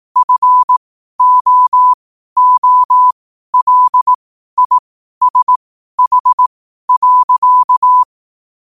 Morse code, heard as a keyed tone, about 18 wpm, spaced as for 12 wpm, 1000 Hz. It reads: FOOLISH.
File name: qotdfiles / 12wpm